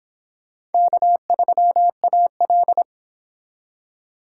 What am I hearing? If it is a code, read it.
Morse code, heard as a keyed tone, about 29 wpm, 700 Hz. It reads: K3AL